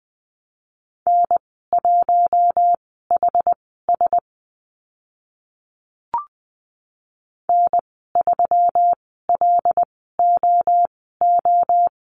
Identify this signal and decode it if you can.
Morse code, heard as a keyed tone, about 20 wpm, spaced as for 14 wpm, 700 Hz. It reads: N1HS N3LOO
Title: 128 Lesson 21g - Number 6 - Call Signs 20wpm